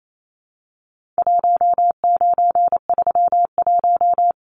Morse code, heard as a keyed tone, about 28 wpm, 700 Hz. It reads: 1931
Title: Numbers - 4-Digit 28wpm